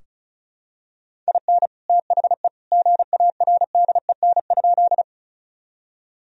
Morse code, heard as a keyed tone, about 36 wpm, 700 Hz. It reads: IN THE GARDEN?
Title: Sentences from Top 1000 Words - Review Part 4 - 35wpm